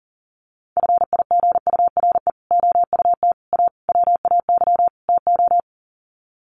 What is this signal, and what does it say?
Morse code, heard as a keyed tone, about 40 wpm, 700 Hz. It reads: FIGURE OUT A WAY TO
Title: Sets of 5 Words Encore2 40wpm